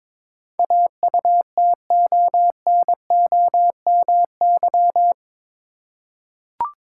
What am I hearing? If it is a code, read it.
Morse code, heard as a keyed tone, about 22 wpm, 700 Hz. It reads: AUTONOMY
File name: practice / 22wpm / Eight Letter Words - 22wpm